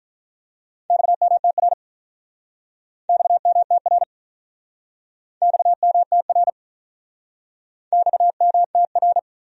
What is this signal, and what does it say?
Morse code, heard as a keyed tone, about 45 wpm, 700 Hz. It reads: XMTR XMTR XMTR XMTR